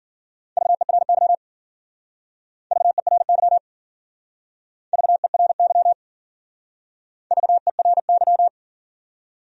Morse code, heard as a keyed tone, about 49 wpm, 700 Hz. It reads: VERY VERY VERY VERY
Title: Speed-Racing Top 500 Words 40wpm